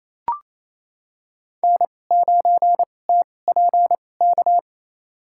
Morse code, heard as a keyed tone, about 28 wpm, spaced as for 20 wpm, 700 Hz. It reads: N9TPK
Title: USA Call Signs - 2x Character Spacing - 28wpm